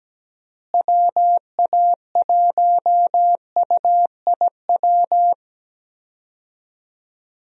Morse code, heard as a keyed tone, about 17 wpm, 700 Hz. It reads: WA1UIW
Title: USA Call Signs - 6 Characters - Repeated 3x - 17wpm